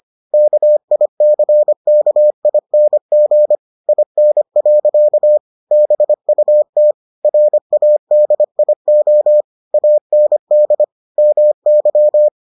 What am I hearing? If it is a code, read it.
Morse code, heard as a keyed tone, about 25 wpm, 600 Hz. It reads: KICKING IN. BUT RADIO AND MY